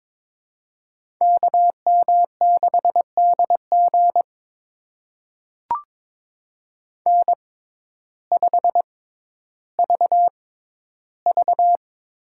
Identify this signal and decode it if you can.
Morse code, heard as a keyed tone, about 22 wpm, spaced as for 12 wpm, 700 Hz. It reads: KM6DG N5VV
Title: USA Call Signs - 5 Characters - 6x Character Spacing - 22wpm